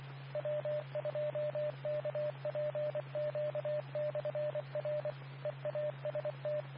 Morse code, heard as a keyed tone, about 24 wpm, 600 Hz, with radio noise. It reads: W2KPQ/R EAST
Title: Nassau and Suffolk County Amateur Repeaters